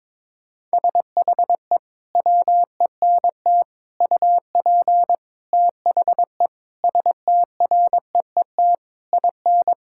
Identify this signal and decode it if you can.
Morse code, heard as a keyed tone, about 22 wpm, 700 Hz. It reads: SHE WENT UP THE STREET IN
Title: Sentences from Top 1000 Words - Review Part 5 - 22wpm